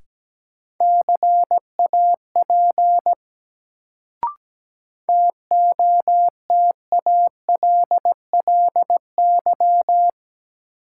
Morse code, heard as a keyed tone, about 17 wpm, 700 Hz. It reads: CAP TOTALLY
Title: Sentences from Top 1600 Words - New Word Drill - 17wpm